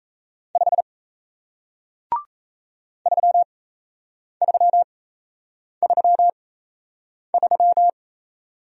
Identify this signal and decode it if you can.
Morse code, heard as a keyed tone, about 37 wpm, spaced as for 8 wpm, 700 Hz. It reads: 5 3333